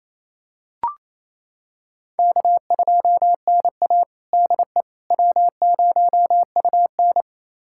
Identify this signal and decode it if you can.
Morse code, heard as a keyed tone, about 28 wpm, 700 Hz. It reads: K2NA DE W0UN
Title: USA Callsign DE Callsign - 4 Characters - 28wpm